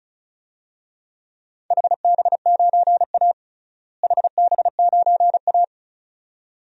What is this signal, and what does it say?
Morse code, heard as a keyed tone, about 36 wpm, 700 Hz. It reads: HB9A HB9A
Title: DXCC-100 Call Signs - 4 Characters - Repeated 2x - 35wpm